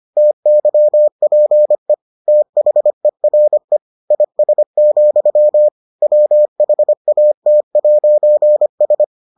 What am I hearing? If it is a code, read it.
Morse code, heard as a keyed tone, about 25 wpm, 600 Hz. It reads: TYPE THERE IS, WHAT'S